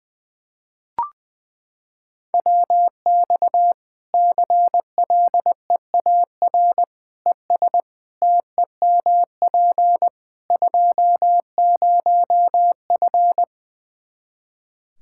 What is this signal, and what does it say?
Morse code, heard as a keyed tone, about 20 wpm, 700 Hz. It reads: WX CLEAR ES TEMP 20F
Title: K1USN SST QSO Practice 20wpm